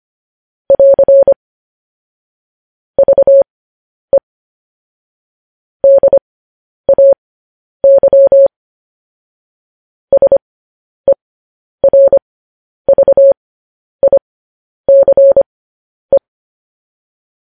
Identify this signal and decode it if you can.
Morse code, heard as a keyed tone, about 25 wpm, spaced as for 10 wpm, 550 Hz. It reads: <AR> VE DAY SERVICE
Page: News Headlines in Morse Code at 10 WPM